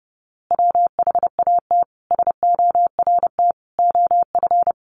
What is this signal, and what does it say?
Morse code, heard as a keyed tone, about 30 wpm, 700 Hz. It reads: WHAT SORT OF
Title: A Sound Of Thunder 30wpm